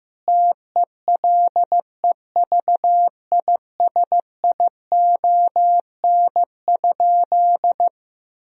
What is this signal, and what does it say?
Morse code, heard as a keyed tone, about 16 wpm, 700 Hz. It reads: TELEVISION?
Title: Sentences from Top 1000 Words - Review Part 4 - 15wpm